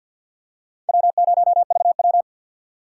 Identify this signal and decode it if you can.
Morse code, heard as a keyed tone, about 50 wpm, 700 Hz. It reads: W0VW
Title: USA Call Signs - 2x Character Spacing - 50wpm